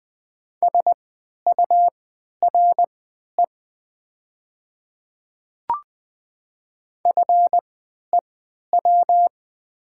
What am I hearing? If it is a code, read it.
Morse code, heard as a keyed tone, about 20 wpm, spaced as for 11 wpm, 700 Hz. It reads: SURE FEW
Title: 3x Character Spacing - Top 300 Words - 20wpm